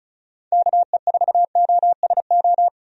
Morse code, heard as a keyed tone, about 36 wpm, 700 Hz. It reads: KE4OSO
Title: USA Call Signs - 6 Characters - Repeated 3x - 35wpm